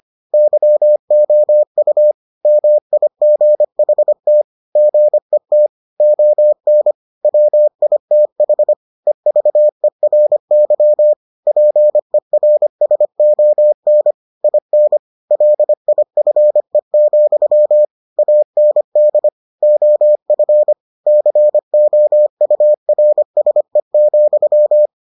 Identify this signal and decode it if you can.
Morse code, heard as a keyed tone, about 25 wpm, 600 Hz. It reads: YOU MIGHT GET ON WITH EVERY PERSON IN LIFE, AND OF COURSE,